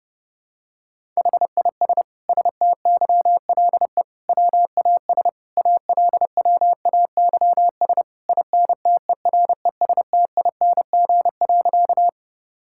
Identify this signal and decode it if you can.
Morse code, heard as a keyed tone, about 32 wpm, 700 Hz. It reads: HIS STYLE WAS ALWAYS INTERESTING.
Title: Sentences from Top 1000 Words - Review Part 5 - 30wpm